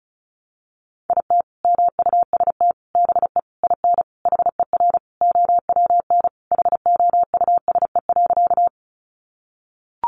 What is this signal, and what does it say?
Morse code, heard as a keyed tone, about 34 wpm, 700 Hz. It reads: IT MUST BE IN HER OWN HOUSE.
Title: Sentences from Top 300 Words 35wpm